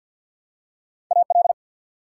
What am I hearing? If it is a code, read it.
Morse code, heard as a keyed tone, about 50 wpm, 700 Hz. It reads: AL
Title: US States and Canadian Provinces 50wpm